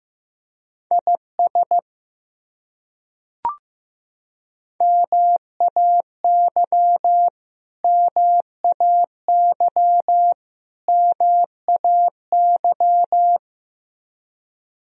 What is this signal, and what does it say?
Morse code, heard as a keyed tone, about 15 wpm, 700 Hz. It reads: IS MAY MAY MAY
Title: Top 100 Words Repeated 15wpm